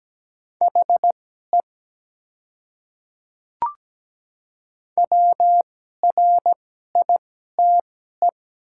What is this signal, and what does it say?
Morse code, heard as a keyed tone, about 17 wpm, spaced as for 12 wpm, 700 Hz. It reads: HE WRITE
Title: 2x Character Spacing - Top 200 Words - 17wpm